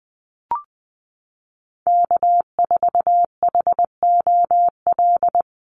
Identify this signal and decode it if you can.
Morse code, heard as a keyed tone, about 20 wpm, 700 Hz. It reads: K4HOL
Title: USA Call Signs Repeated 3x 20wpm